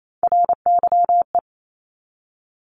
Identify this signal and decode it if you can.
Morse code, heard as a keyed tone, about 28 wpm, 700 Hz. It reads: RYE
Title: Multi-letter Phonemes 28wpm